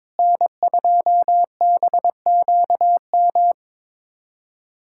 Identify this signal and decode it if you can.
Morse code, heard as a keyed tone, about 22 wpm, 700 Hz. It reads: N2BQM
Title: USA Callsign DE Callsign - 5 Characters - 22wpm